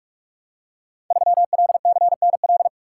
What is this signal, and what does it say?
Morse code, heard as a keyed tone, about 44 wpm, 700 Hz. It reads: 3LCNL
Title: Code Groups - 5 Letter:Numbers - 45wpm